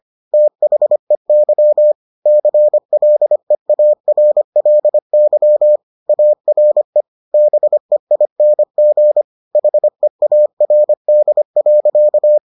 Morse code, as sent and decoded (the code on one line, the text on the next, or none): - .... -.-- / -.-. .-.. .- .-. .-.. -.-- / .- .-. / -... .. -. --. / .... .- .-. -.. .-.-.-
THEY CLEARLY ARE BEING HEARD.